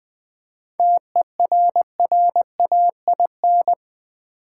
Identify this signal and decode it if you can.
Morse code, heard as a keyed tone, about 20 wpm, 700 Hz. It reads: TERRAIN
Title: 7 Letter Words - First 3 Letters Given - 20wpm